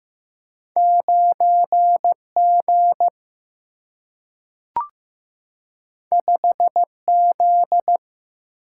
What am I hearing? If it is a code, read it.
Morse code, heard as a keyed tone, about 15 wpm, 700 Hz. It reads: 9G 5Z
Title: Single-Letter Single-Number 15wpm